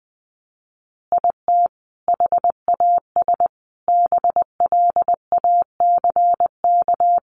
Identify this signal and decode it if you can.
Morse code, heard as a keyed tone, about 20 wpm, 700 Hz. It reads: IT HAS BLACK
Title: Additional Easy Phrases 20wpm